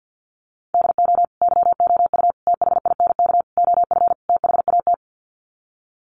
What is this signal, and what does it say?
Morse code, heard as a keyed tone, about 50 wpm, 700 Hz. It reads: DO YOU THINK OF THAT
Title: Sets of 5 Words Encore2 50wpm